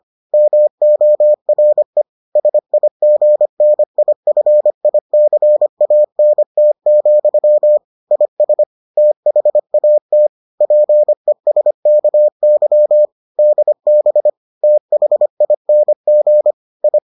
Morse code, heard as a keyed tone, about 25 wpm, 600 Hz. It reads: MORE SIGNIFICANT, IS THAT PESKY DB THING I